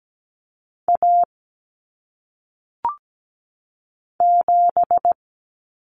Morse code, heard as a keyed tone, about 17 wpm, 700 Hz. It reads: A 7
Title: Common Cut Numbers - 1-Digit 17wpm